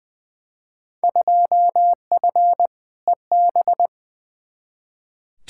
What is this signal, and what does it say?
Morse code, heard as a keyed tone, about 20 wpm, 700 Hz. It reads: 2F EB
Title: Field Day Exchanges Encore - 20wpm